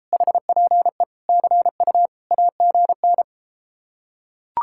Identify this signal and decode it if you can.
Morse code, heard as a keyed tone, about 33 wpm, 700 Hz. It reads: HPE CU AGN